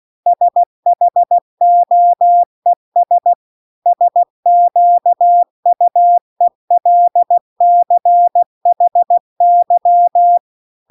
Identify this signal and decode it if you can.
Morse code, heard as a keyed tone, about 16 wpm, 700 Hz. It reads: SHOES SQUELCHY